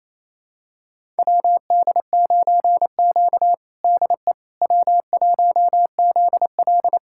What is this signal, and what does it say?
Morse code, heard as a keyed tone, about 28 wpm, 700 Hz. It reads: WD9Q DE W1ZL
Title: USA Callsign DE Callsign - 4 Characters - 28wpm